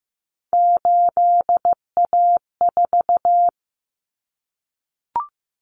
Morse code, heard as a keyed tone, about 15 wpm, 700 Hz. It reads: 8A4
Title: Common Cut Numbers - 3-Digit 15wpm